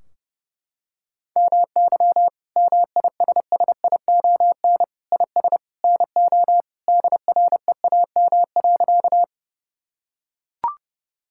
Morse code, heard as a keyed tone, about 30 wpm, 700 Hz. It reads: MY MISSION IS NO DREAM.
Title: Sentences from Top 1600 Words 30wpm